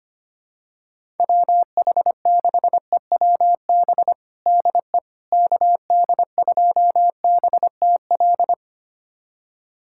Morse code, heard as a keyed tone, about 25 wpm, 700 Hz. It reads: WH6EWB DE KD2BTL
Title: USA Callsign DE Callsign - 6 Characters - 25wpm